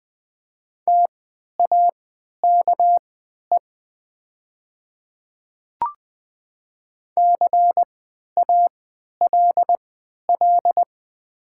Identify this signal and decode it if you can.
Morse code, heard as a keyed tone, about 20 wpm, spaced as for 11 wpm, 700 Hz. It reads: TAKE CALL
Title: 3x Character Spacing - Top 300 Words - 20wpm